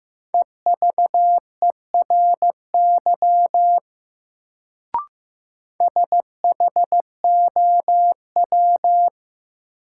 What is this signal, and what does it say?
Morse code, heard as a keyed tone, about 15 wpm, 700 Hz. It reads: EVERY SHOW
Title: Sets of 5 Words - New Word Drill - 15wpm